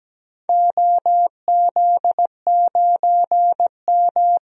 Morse code, heard as a keyed tone, about 17 wpm, 700 Hz. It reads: OZ9M
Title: DXCC-100 Call Signs - 4 Characters - 17wpm